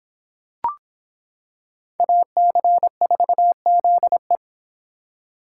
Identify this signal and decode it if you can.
Morse code, heard as a keyed tone, about 26 wpm, 700 Hz. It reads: AC4ZE